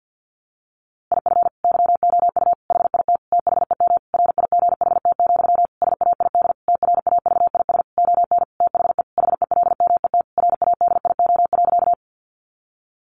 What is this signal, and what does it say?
Morse code, heard as a keyed tone, about 50 wpm, 700 Hz. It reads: IF YOU HIT THEM RIGHT, SAID TRAVIS ON THE HELMET RADIO.
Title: A Sound Of Thunder 50wpm